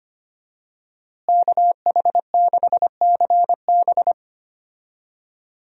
Morse code, heard as a keyed tone, about 25 wpm, 700 Hz. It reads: KH6CB
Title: USA Call Signs - 5 Characters - Repeated 2x - 25wpm